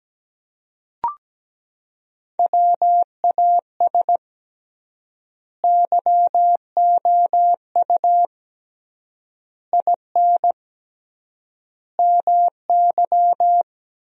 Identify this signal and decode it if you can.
Morse code, heard as a keyed tone, about 17 wpm, 700 Hz. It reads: WAS YOU IN MY
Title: 3x Word-Spacing - Sentences from Top 300 Words - 17wpm